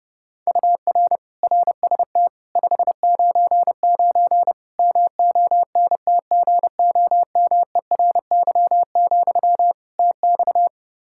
Morse code, heard as a keyed tone, about 30 wpm, 700 Hz. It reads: UR RST 599 MONTGOMERY, TX
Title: K1USN SST QSO Practice 30wpm